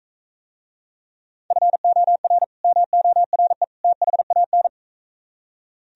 Morse code, heard as a keyed tone, about 44 wpm, 700 Hz. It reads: FOR MORE THAN